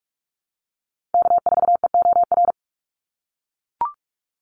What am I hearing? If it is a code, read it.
Morse code, heard as a keyed tone, about 44 wpm, 700 Hz. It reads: K4EOR
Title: Call Signs - 45wpm